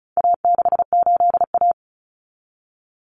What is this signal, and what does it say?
Morse code, heard as a keyed tone, about 34 wpm, 700 Hz. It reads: A68A
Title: Common Cut Numbers - 4-Digit 35wpm